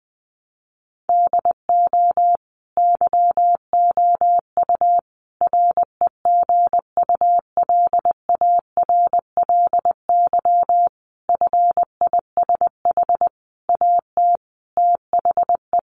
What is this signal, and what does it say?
Morse code, heard as a keyed tone, about 19 wpm, 700 Hz. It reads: DO YOU REGULARLY FISH AT THE
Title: Sentences from Top 1400 Words 20wpm